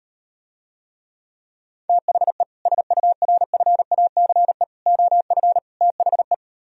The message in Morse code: THE SURFACE OF THE

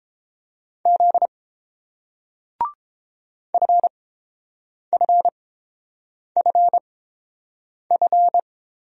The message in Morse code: Z FFFF

